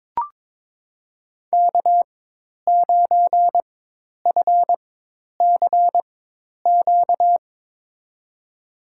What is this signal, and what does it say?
Morse code, heard as a keyed tone, about 22 wpm, spaced as for 10 wpm, 700 Hz. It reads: K9FCQ
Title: USA Call Signs - 5 Characters - 4x Character Spacing - 22wpm